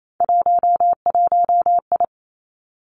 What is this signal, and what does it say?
Morse code, heard as a keyed tone, about 28 wpm, 700 Hz. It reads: 11I
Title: Three Challenging Letter-Number Combinations 28wpm